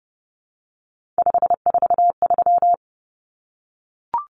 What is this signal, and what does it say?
Morse code, heard as a keyed tone, about 30 wpm, 700 Hz. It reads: 543
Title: Numbers - 3-Digit 30wpm